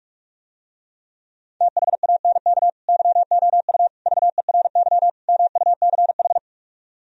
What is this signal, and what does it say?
Morse code, heard as a keyed tone, about 44 wpm, 700 Hz. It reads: THANK YOU VERY MUCH